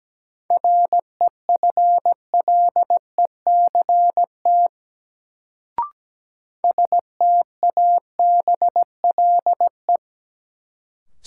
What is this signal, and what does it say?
Morse code, heard as a keyed tone, about 17 wpm, 700 Hz. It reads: REFLECT STABLE
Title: Sentences from Top 1500 Words - New Word Drill Alt - 17wpm